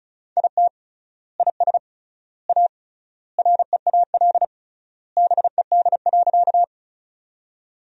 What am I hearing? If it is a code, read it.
Morse code, heard as a keyed tone, about 34 wpm, 700 Hz. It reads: IT IS A REAL BED.
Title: 3x Word-Spacing - Sentences from Top 500 Words - 35wpm